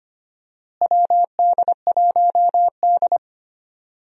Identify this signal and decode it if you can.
Morse code, heard as a keyed tone, about 25 wpm, 700 Hz. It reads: WD1D
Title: USA Call Signs - 4 Characters - Repeated 3x - 25wpm